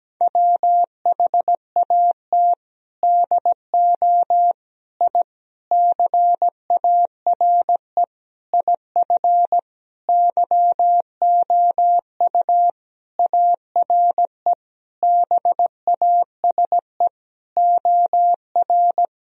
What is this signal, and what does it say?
Morse code, heard as a keyed tone, about 17 wpm, 700 Hz. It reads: WHAT DO I CARE IF YOU ARE BASE OR
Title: Sentences from Top 1000 Words - Review Part 1 - 17wpm